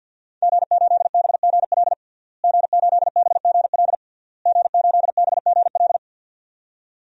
Morse code, heard as a keyed tone, about 47 wpm, 700 Hz. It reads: G8BGL G8BGL G8BGL
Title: DXCC-100 Call Signs - 5 Characters - Repeated 3x - 50wpm